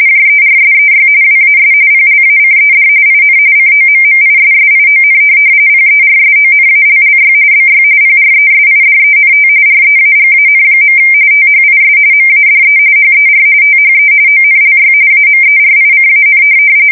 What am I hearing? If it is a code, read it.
RTTY, about 45.45 baud. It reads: JERSEY WAS PLACED IN COMMISSION PERSONALLY BY PRESIDENT REAGAN. THIS MADE HER THE ONLY AMERICAN
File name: fsk170 / USS-New-Jersey